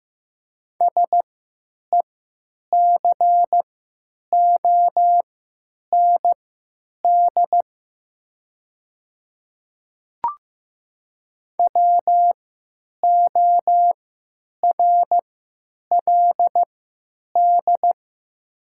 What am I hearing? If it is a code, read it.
Morse code, heard as a keyed tone, about 15 wpm, spaced as for 9 wpm, 700 Hz. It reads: SECOND WORLD
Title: 3x Character Spacing - Top 300 Words - 15wpm